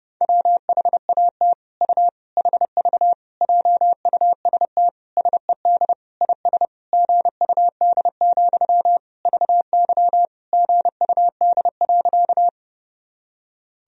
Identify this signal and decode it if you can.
Morse code, heard as a keyed tone, about 30 wpm, 700 Hz. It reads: WHAT U HV JUST SED IS GUD, VY GUD.
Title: Sentences from Top 200 Words and Abbreviations 30wpm